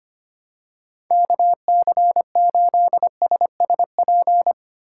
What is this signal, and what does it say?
Morse code, heard as a keyed tone, about 25 wpm, 700 Hz. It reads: KC8SSP
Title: USA Call Signs - 6 Characters - 2x Character Spacing - 25wpm